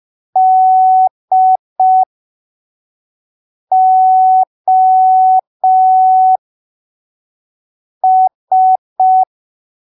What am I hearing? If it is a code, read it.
Morse code, heard as a keyed tone, about 5 wpm, spaced as for 3 wpm, 750 Hz. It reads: DOS